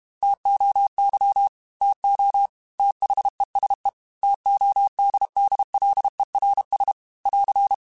Morse code, heard as a keyed tone, about 31 wpm, 800 Hz. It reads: TOY TO THESE TODDLERS <AR>
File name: T Runs / T Runs mp3s / T Runs @30